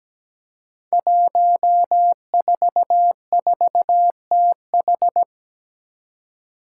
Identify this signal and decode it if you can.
Morse code, heard as a keyed tone, about 17 wpm, 700 Hz. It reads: 144TH
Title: Five Challenging Letter-Number Combinations 17wpm